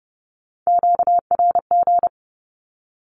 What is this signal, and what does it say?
Morse code, heard as a keyed tone, about 30 wpm, 700 Hz. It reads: QRG
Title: Q-Codes with Spoken Meaning - 30wpm